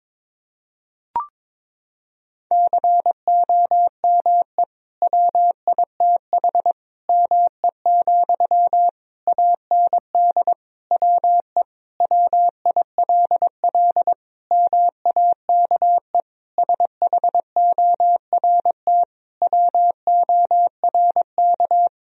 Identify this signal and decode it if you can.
Morse code, heard as a keyed tone, about 21 wpm, 700 Hz. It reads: COME WITH ME, AND WE WILL MAKE SHORT WORK
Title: Sentences from Top 1000 Words - Review Part 2 - 22wpm